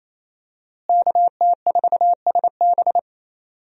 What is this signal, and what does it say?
Morse code, heard as a keyed tone, about 28 wpm, 700 Hz. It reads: KT4SB
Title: USA Call Signs - 5 Characters - Repeated 3x - 28wpm